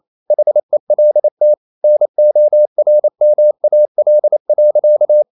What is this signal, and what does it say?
Morse code, heard as a keyed tone, about 28 wpm, 600 Hz. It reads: HELT NORMAL.
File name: Ferie QRQ 28wpm